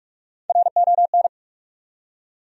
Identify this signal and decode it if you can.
Morse code, heard as a keyed tone, about 46 wpm, 700 Hz. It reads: RON
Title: USA Common Names - Max 3 Letters - 45wpm